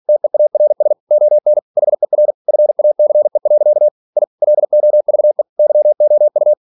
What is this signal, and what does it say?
Morse code, heard as a keyed tone, about 46 wpm, 600 Hz. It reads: TEARS ON HER FACE. I LOVE YOU
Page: Never 47wpm